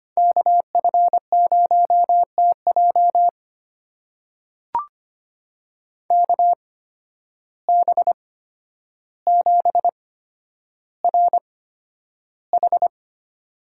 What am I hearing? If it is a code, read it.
Morse code, heard as a keyed tone, about 25 wpm, spaced as for 11 wpm, 700 Hz. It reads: KF0TJ KB7RH
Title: USA Call Signs - 5 Characters - 8x Character Spacing - 25wpm